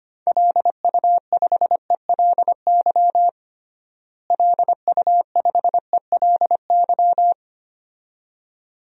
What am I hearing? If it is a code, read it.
Morse code, heard as a keyed tone, about 25 wpm, 700 Hz. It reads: LU5ELY LU5ELY
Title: DXCC-100 Call Signs - 6 Characters - Repeated 2x - 25wpm